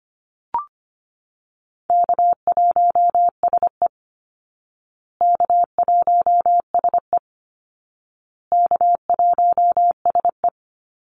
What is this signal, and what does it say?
Morse code, heard as a keyed tone, about 25 wpm, 700 Hz. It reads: K1SE K1SE K1SE
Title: Easy Call Signs Repeated 3x - 25wpm